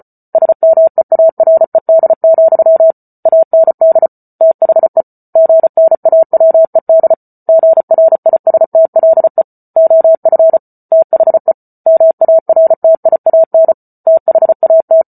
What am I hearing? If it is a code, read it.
Morse code, heard as a keyed tone, about 34 wpm, 650 Hz. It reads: SMEARED, AND THE GNAWED GRISTLE OF THE MARTIAN THAT